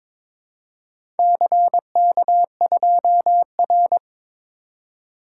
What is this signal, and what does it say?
Morse code, heard as a keyed tone, about 22 wpm, 700 Hz. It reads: CK2R
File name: practice / 22wpm / Four Letter-Number Combinations - 22wpm